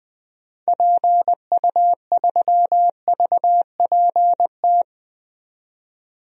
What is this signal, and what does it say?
Morse code, heard as a keyed tone, about 20 wpm, 700 Hz. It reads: PU3VPT
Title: DXCC-100 Call Signs - 6 Characters - 20wpm